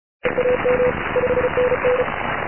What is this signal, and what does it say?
Morse code, heard as a keyed tone, about 36 wpm, 500 Hz, with radio noise. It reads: EAM 5NN
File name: Recordings / vp6dx 80m cw ssb 18 2 08 0740 edit